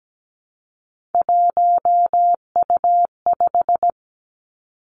Morse code, heard as a keyed tone, about 17 wpm, 700 Hz. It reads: 1U5
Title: Three Challenging Letter-Number Combinations 17wpm